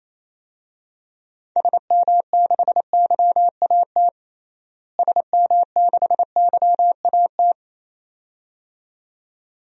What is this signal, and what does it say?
Morse code, heard as a keyed tone, about 27 wpm, 700 Hz. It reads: SM6YAT SM6YAT
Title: DXCC-100 Call Signs - 6 Characters - Repeated 2x - 28wpm